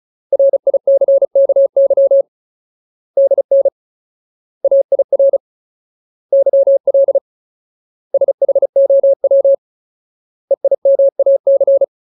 Morse code, heard as a keyed tone, about 34 wpm, 550 Hz. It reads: RICKY DN AIR YL SHOW EIMAC
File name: Ultimate Word List / 35WPM Ultimate